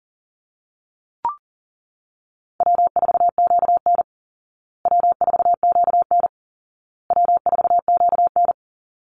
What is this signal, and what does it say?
Morse code, heard as a keyed tone, about 40 wpm, 700 Hz. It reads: W4QN W4QN W4QN
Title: Easy Call Signs Repeated 3x - 40wpm